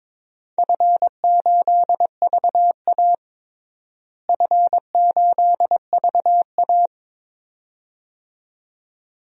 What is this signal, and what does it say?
Morse code, heard as a keyed tone, about 22 wpm, 700 Hz. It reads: F8VA F8VA
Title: DXCC-100 Call Signs - 4 Characters - Repeated 2x - 22wpm